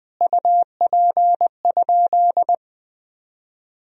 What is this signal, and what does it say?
Morse code, heard as a keyed tone, about 20 wpm, 700 Hz. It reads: UP?
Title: Sentences from Top 1000 Words - Review Part 1 - 20wpm